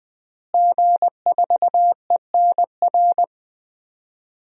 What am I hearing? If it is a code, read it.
Morse code, heard as a keyed tone, about 20 wpm, 700 Hz. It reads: G4ENR
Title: DXCC-100 Call Signs - 5 Characters - Repeated 3x - 20wpm